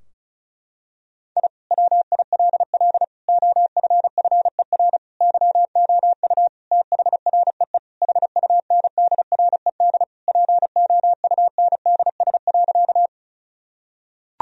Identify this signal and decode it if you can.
Morse code, heard as a keyed tone, about 36 wpm, 700 Hz. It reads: I WILL OFFER YOU THREE HUNDRED POUNDS.
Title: Sentences from Top 1000 Words - Review Part 4 - 35wpm